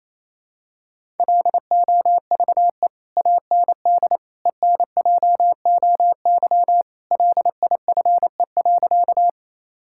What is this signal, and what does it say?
Morse code, heard as a keyed tone, about 28 wpm, 700 Hz. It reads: LOVE AND ENJOY LIFE.
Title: Sentences from Top 1000 Words - Review Part 5 - 28wpm